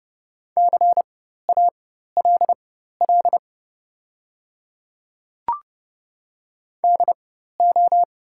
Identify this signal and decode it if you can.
Morse code, heard as a keyed tone, about 30 wpm, spaced as for 14 wpm, 700 Hz. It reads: CALL DO
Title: 4x Character Spacing - Top 100 Words - 30wpm